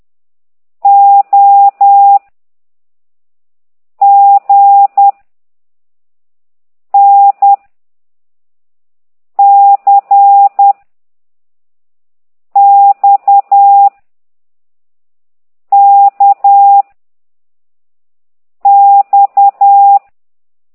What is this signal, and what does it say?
Morse code, heard as a keyed tone, about 10 wpm, 800 Hz. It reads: O G N C X K X
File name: ct2idw / exerc11